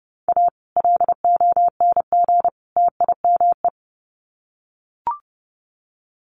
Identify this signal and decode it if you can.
Morse code, heard as a keyed tone, about 30 wpm, 700 Hz. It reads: A LONG TIME
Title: Sets of 3 Words 30wpm